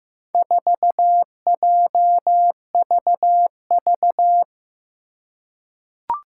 Morse code, heard as a keyed tone, about 15 wpm, 700 Hz. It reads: 4JVV
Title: Four Challenging Letter-Number Combinations 15wpm